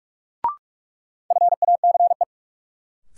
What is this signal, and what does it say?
Morse code, heard as a keyed tone, about 46 wpm, 700 Hz. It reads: FACE
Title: Sets of 3 Words - New Word Drill - 45wpm